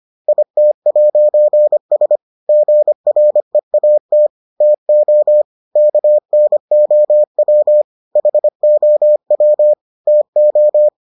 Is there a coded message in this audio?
Yes: Morse code from 0.3 to 10.9 s, heard as a keyed tone, about 25 wpm, 600 Hz. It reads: IT'S GREAT TO KNOW HOW TO